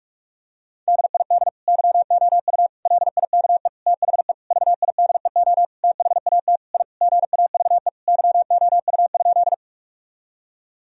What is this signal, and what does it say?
Morse code, heard as a keyed tone, about 46 wpm, 700 Hz. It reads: DID YOU LIKE THE VIDEO THAT I GAVE YOU?
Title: Sentences from Top 1000 Words - Review Part 4 - 45wpm